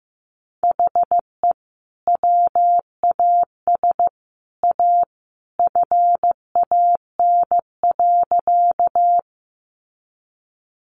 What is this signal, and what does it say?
Morse code, heard as a keyed tone, about 15 wpm, 700 Hz. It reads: HE WAS A FAN.
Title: Sentences from Top 1300 Words 15wpm